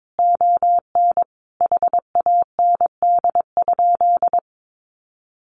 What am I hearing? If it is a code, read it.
Morse code, heard as a keyed tone, about 22 wpm, 700 Hz. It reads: ON HAND?
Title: Sentences from Top 1500 Words 22wpm